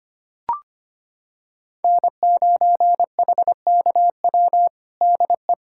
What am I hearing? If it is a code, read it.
Morse code, heard as a keyed tone, about 25 wpm, 700 Hz. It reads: N9HKW DE
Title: USA Callsign DE Callsign - 5 Characters - 25wpm